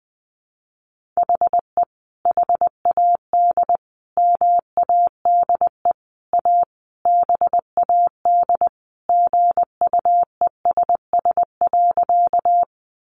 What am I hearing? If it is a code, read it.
Morse code, heard as a keyed tone, about 20 wpm, 700 Hz. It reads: HE HAD MADE A BAD GUESS.
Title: Sentences from Top 1500 Words 20wpm